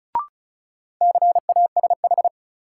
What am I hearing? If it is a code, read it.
Morse code, heard as a keyed tone, about 36 wpm, 700 Hz. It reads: CASH
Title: Sentences from Top 700 Words - New Word Drill - 35wpm